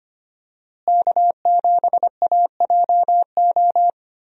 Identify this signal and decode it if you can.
Morse code, heard as a keyed tone, about 25 wpm, 700 Hz. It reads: K7AJO
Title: USA Call Signs - 5 Characters - Repeated 3x - 25wpm